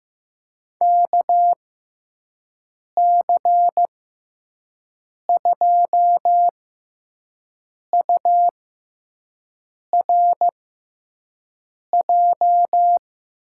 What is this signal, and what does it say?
Morse code, heard as a keyed tone, about 15 wpm, spaced as for 5 wpm, 700 Hz. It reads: KC2URJ
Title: USA Call Signs - 6 Characters - 6x Character Spacing - 15wpm